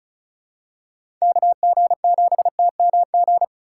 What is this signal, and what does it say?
Morse code, heard as a keyed tone, about 36 wpm, 700 Hz. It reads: KG7TMG
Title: USA Call Signs - 6 Characters - 4x Character Spacing - 35wpm